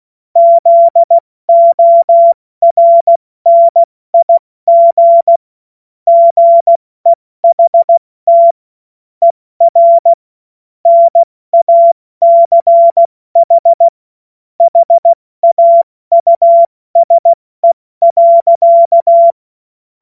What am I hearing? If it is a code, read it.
Morse code, heard as a keyed tone, about 16 wpm, 680 Hz. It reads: ZORNIG GEHT ER NACH HAUSE.
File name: Witz-08-Juli-2025-70-F80